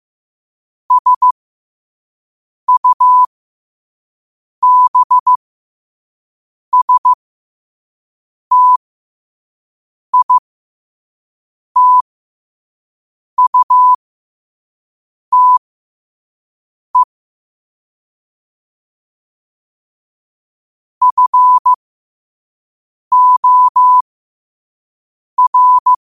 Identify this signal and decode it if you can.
Morse code, heard as a keyed tone, about 15 wpm, spaced as for 5 wpm, 1000 Hz. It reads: SUBSTITUTE FOR
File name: qotdfiles / 5wpm